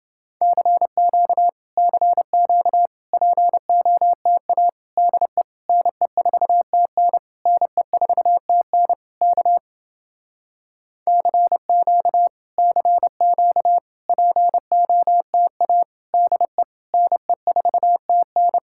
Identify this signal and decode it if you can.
Morse code, heard as a keyed tone, about 30 wpm, 700 Hz. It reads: CQ CQ POTA DE NE4TN NE4TN K CQ CQ POTA DE NE4TN